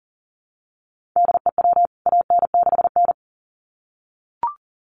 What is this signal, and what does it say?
Morse code, heard as a keyed tone, about 40 wpm, 700 Hz. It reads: DEW AN6N
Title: CWT Exchange 40wpm